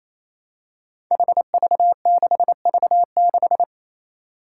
Morse code, heard as a keyed tone, about 28 wpm, 700 Hz. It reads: HV6V6
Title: Five Challenging Letter-Number Combinations 28wpm